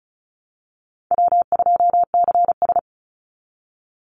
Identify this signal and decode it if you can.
Morse code, heard as a keyed tone, about 34 wpm, 700 Hz. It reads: W2CS
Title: Easy Call Signs Repeated 2x - 35wpm